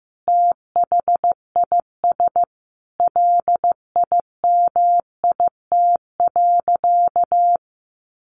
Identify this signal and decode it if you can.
Morse code, heard as a keyed tone, about 14 wpm, 700 Hz. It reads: THIS LIMIT.
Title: Sentences from Top 1500 Words 15wpm